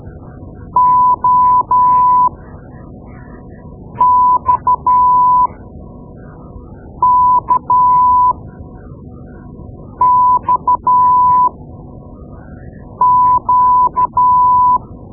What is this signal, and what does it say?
Morse code, heard as a keyed tone, about 12 wpm, spaced as for 4 wpm, 1000 Hz. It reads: OXKXQ
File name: Morse Lessons / Morse Lessons / Tape-2B